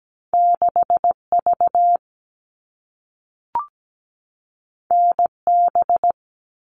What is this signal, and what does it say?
Morse code, heard as a keyed tone, about 17 wpm, 700 Hz. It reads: 6V NB
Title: All Cut Numbers - 2-Digit 17wpm